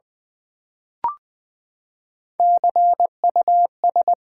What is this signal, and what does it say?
Morse code, heard as a keyed tone, about 20 wpm, 700 Hz. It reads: CUS
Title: Build A Word - 3 Letters At A Time - 6-Letter Words - 20wpm